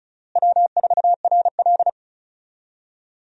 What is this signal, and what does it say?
Morse code, heard as a keyed tone, about 34 wpm, 700 Hz. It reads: W4RL
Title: USA Call Signs - 4 Characters - Repeated 2x - 35wpm